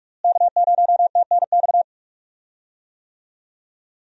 Morse code, heard as a keyed tone, about 46 wpm, 700 Hz. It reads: K0TNX
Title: USA Call Signs - 5 Characters - Repeated 3x - 45wpm